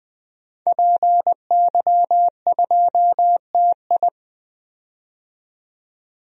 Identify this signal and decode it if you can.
Morse code, heard as a keyed tone, about 21 wpm, 700 Hz. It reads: PY2TI
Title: DXCC-100 Call Signs - 5 Characters - 20wpm